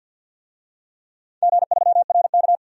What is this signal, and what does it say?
Morse code, heard as a keyed tone, about 52 wpm, 700 Hz. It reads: G3RK
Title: DXCC-100 Call Signs - 4 Characters - 50wpm